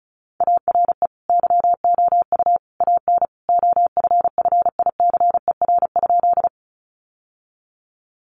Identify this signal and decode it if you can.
Morse code, heard as a keyed tone, about 34 wpm, 700 Hz. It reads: ARE YOU AN OFFICER?
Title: Sentences from Top 1200 Words 35wpm